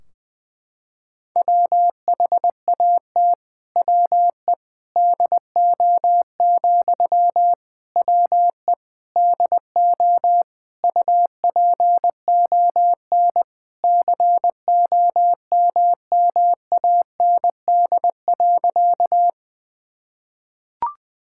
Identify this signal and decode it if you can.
Morse code, heard as a keyed tone, about 20 wpm, 700 Hz. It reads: WHAT WE DO, WE DO UPON COMMAND.
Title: Sentences from Top 1600 Words 20wpm